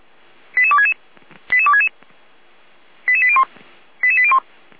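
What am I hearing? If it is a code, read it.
Five-tone selective call (ZVEI), heard as quick runs of five tones, tones of about 70 ms. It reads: 88266 88266 88801 88801
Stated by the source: ZVEI five-tone selective call received on an SDR.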